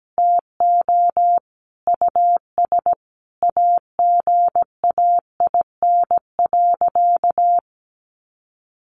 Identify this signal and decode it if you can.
Morse code, heard as a keyed tone, about 17 wpm, 700 Hz. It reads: TO US AGAIN.
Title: Sentences from Top 1100 Words 17wpm